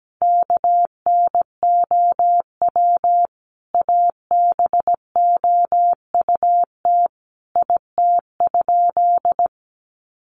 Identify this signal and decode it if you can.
Morse code, heard as a keyed tone, about 17 wpm, 700 Hz. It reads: KNOW ABOUT IT?
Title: Sentences from Top 100 Words 17wpm